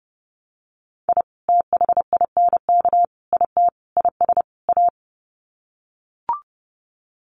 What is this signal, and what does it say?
Morse code, heard as a keyed tone, about 30 wpm, 700 Hz. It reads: I THINK IT IS A
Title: Sets of 5 Words Encore 30wpm